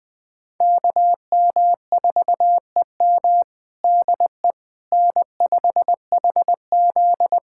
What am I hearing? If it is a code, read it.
Morse code, heard as a keyed tone, about 20 wpm, 700 Hz. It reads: KM4EM DE N5HZ
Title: Practice QSO - Exchange 4 - Ending - 20wpm